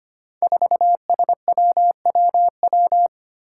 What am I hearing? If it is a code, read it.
Morse code, heard as a keyed tone, about 25 wpm, 700 Hz. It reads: 4SWWW
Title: Five Challenging Letter-Number Combinations 25wpm